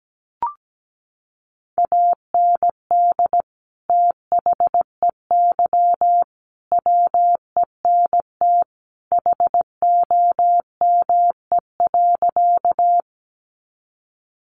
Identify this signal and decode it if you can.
Morse code, heard as a keyed tone, about 17 wpm, 700 Hz. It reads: AND THEY WENT HOME.
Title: Sentences from Top 200 Words 17wpm